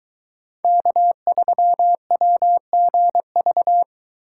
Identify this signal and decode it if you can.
Morse code, heard as a keyed tone, about 23 wpm, 700 Hz. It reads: K3WGV